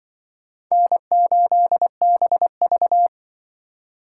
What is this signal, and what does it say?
Morse code, heard as a keyed tone, about 25 wpm, 700 Hz. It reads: N8BV